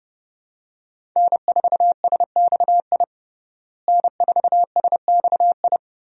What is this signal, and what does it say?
Morse code, heard as a keyed tone, about 29 wpm, 700 Hz. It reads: N4SXI N4SXI
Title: USA Call Signs - 5 Characters - Repeated 3x - 30wpm